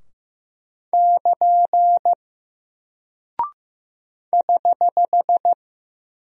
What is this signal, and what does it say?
Morse code, heard as a keyed tone, about 15 wpm, 700 Hz. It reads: <KN> <HH>
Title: Single Character (Letter:Number:Punctuation:Prosign) Alt 15wpm